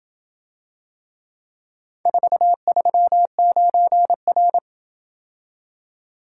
Morse code, heard as a keyed tone, about 27 wpm, 700 Hz. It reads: 439R